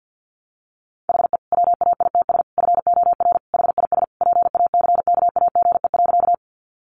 Morse code, heard as a keyed tone, about 50 wpm, 700 Hz. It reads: HE WAITS FOR HIS PACKAGE.
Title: Additional Easy Phrases 50wpm